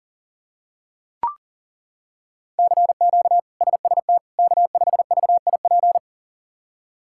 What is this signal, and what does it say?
Morse code, heard as a keyed tone, about 37 wpm, 700 Hz. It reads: CQ SST K5VIP
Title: K1USN SST QSO Practice - 40wpm